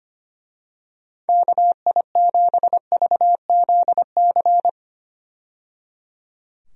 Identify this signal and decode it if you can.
Morse code, heard as a keyed tone, about 25 wpm, 700 Hz. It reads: KI7VZC
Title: USA Call Signs - 6 Characters - 25wpm